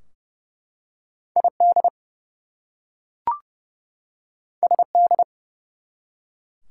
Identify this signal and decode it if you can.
Morse code, heard as a keyed tone, about 30 wpm, 700 Hz. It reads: ID SD
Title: US States and Canadian Provinces 30wpm